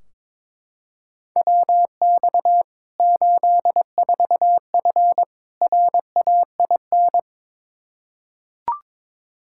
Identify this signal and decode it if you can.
Morse code, heard as a keyed tone, about 21 wpm, 700 Hz. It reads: WX 84F RAIN
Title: K1USN SST QSO Practice 22wpm